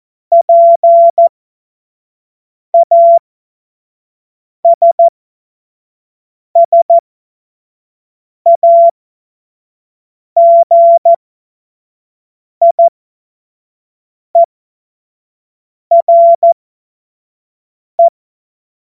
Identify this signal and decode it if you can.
Morse code, heard as a keyed tone, about 14 wpm, spaced as for 5 wpm, 680 Hz. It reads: PASSAGIERE
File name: Morsekurs / Witze / Witz 20. Juni 2023 25 F70